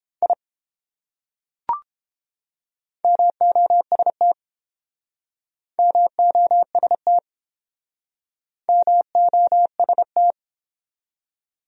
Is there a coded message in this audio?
Yes: Morse code from 0.2 to 10.3 s, heard as a keyed tone, about 28 wpm, 700 Hz. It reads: I MOST MOST MOST